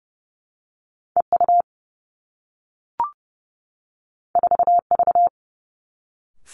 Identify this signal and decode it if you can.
Morse code, heard as a keyed tone, about 30 wpm, 700 Hz. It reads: E U 4 V
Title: Two Challenging Letter-Number Combinations 30wpm